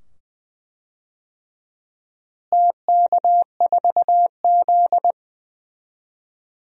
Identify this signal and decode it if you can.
Morse code, heard as a keyed tone, about 21 wpm, 700 Hz. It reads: TK4Z
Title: DXCC-100 Call Signs - 4 Characters - Repeated 2x - 20wpm